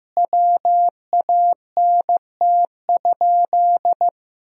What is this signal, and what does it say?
Morse code, heard as a keyed tone, about 14 wpm, 700 Hz. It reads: WANT?
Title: 2x Word-Spacing - Sentences from Top 200 Words - 15wpm